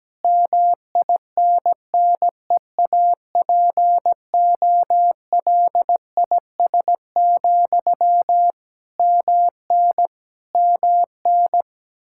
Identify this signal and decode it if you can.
Morse code, heard as a keyed tone, about 17 wpm, 700 Hz. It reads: MINNEAPOLIS, MN MN
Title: Practice QSO - Exchange 1 - Beginning - 17wpm